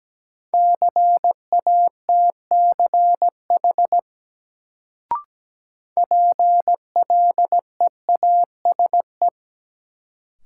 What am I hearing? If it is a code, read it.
Morse code, heard as a keyed tone, about 17 wpm, 700 Hz. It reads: CATCH PLEASE
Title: Sentences from Top 1500 Words - New Word Drill Alt - 17wpm